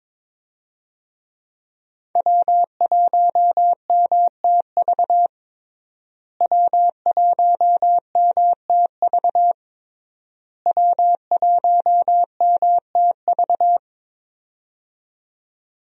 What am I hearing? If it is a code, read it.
Morse code, heard as a keyed tone, about 22 wpm, 700 Hz. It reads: W1MTV W1MTV W1MTV
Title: USA Call Signs - 5 Characters - Repeated 3x - 22wpm